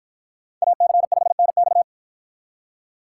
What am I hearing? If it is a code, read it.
Morse code, heard as a keyed tone, about 48 wpm, 700 Hz. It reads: AC5NX